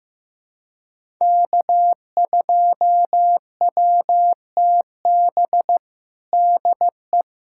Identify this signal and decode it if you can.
Morse code, heard as a keyed tone, about 16 wpm, 700 Hz. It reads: K2WTB DE
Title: USA Callsign DE Callsign - 5 Characters - 15wpm